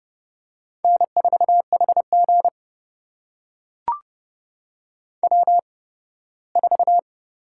Morse code, heard as a keyed tone, about 30 wpm, spaced as for 18 wpm, 700 Hz. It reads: N4HG W4
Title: USA Call Signs - 4 Characters - 8x Character Spacing - 30wpm